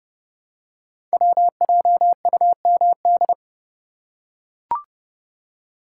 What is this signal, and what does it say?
Morse code, heard as a keyed tone, about 29 wpm, 700 Hz. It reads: WJUMD
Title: Code Groups - 5 Letters - 30wpm